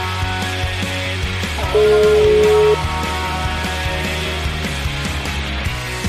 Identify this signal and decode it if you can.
Ringback tone (Europe, 425 Hz), with noise.